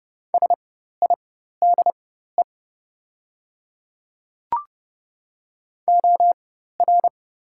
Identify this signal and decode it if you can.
Morse code, heard as a keyed tone, about 30 wpm, spaced as for 14 wpm, 700 Hz. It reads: SIDE OR
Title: 4x Character Spacing - Top 500 Words - 30wpm